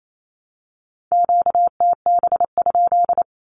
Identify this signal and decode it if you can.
Morse code, heard as a keyed tone, about 28 wpm, 700 Hz. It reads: QTB?
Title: Q-Codes Repeated 3x 28wpm